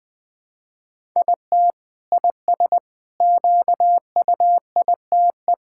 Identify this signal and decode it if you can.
Morse code, heard as a keyed tone, about 20 wpm, 700 Hz. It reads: IT IS QUITE
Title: Sentences from Top 1000 Words - Review Part 5 - 20wpm